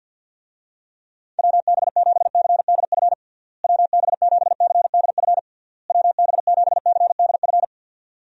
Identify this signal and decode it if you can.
Morse code, heard as a keyed tone, about 50 wpm, 700 Hz. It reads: WB7CDF WB7CDF WB7CDF
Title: USA Call Signs - 6 Characters - Repeated 3x - 50wpm